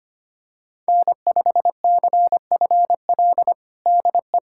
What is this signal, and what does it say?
Morse code, heard as a keyed tone, about 25 wpm, 700 Hz. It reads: N5CFL DE
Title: USA Callsign DE Callsign - 5 Characters - 25wpm